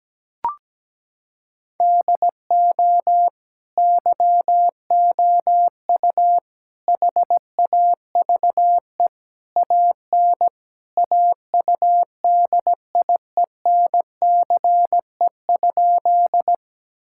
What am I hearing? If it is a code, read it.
Morse code, heard as a keyed tone, about 17 wpm, 700 Hz. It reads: DO YOU HAVE AN AUDIENCE?
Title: Sentences from Top 1000 Words - Review Part 5 - 17wpm